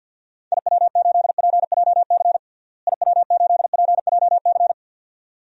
Morse code, heard as a keyed tone, about 50 wpm, 700 Hz. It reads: IW8PJC IW8PJC
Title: DXCC-100 Call Signs - 6 Characters - Repeated 2x - 50wpm